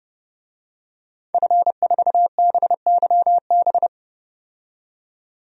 Morse code, heard as a keyed tone, about 30 wpm, 700 Hz. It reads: F4BYB
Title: DXCC-100 Call Signs - 5 Characters - 30wpm